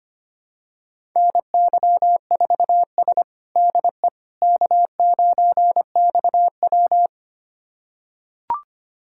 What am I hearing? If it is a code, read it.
Morse code, heard as a keyed tone, about 25 wpm, 700 Hz. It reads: NY4S DE K9XW
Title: USA Callsign DE Callsign - 4 Characters - 25wpm